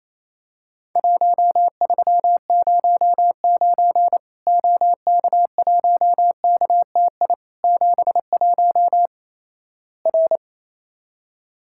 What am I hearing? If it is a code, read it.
Morse code, heard as a keyed tone, about 28 wpm, 700 Hz. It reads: 1309 OK1KTI 71 R
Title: Worked All Europe DX Contest - QTC Exchange - 28wpm